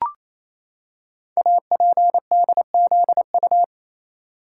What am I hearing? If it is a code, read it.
Morse code, heard as a keyed tone, about 28 wpm, 700 Hz. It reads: APDZU
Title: Code Groups - 5 Letters - 28wpm